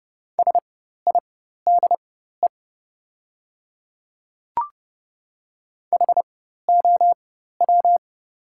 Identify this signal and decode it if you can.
Morse code, heard as a keyed tone, about 30 wpm, spaced as for 14 wpm, 700 Hz. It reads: SIDE HOW
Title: 4x Character Spacing - Top 100 Words - 30wpm